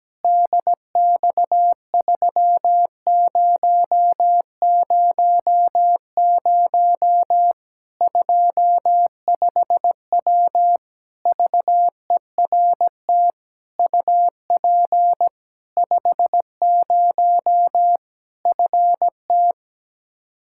Morse code, heard as a keyed tone, about 17 wpm, 700 Hz. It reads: DX3000 25W VERT UP 50 FT
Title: Practice QSO - Exchange 2 - Info - 17wpm